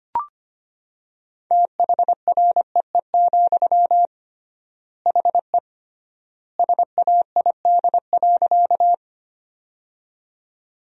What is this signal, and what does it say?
Morse code, heard as a keyed tone, about 25 wpm, 700 Hz. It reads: THREE, HE SAID.
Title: 3x Word-Spacing - Sentences from Top 200 Words - 25wpm